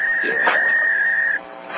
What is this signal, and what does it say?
Fax machine handshake.